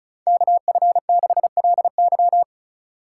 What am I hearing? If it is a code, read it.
Morse code, heard as a keyed tone, about 34 wpm, 700 Hz. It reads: KF6LY
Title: USA Call Signs - 5 Characters - 3x Character Spacing - 35wpm